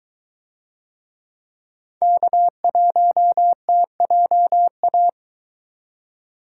Morse code, heard as a keyed tone, about 22 wpm, 700 Hz. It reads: K1TJA